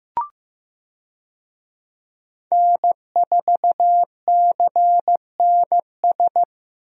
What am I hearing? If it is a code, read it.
Morse code, heard as a keyed tone, about 15 wpm, 700 Hz. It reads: N4CNS
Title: USA Call Signs - 5 Characters - Repeated 2x - 15wpm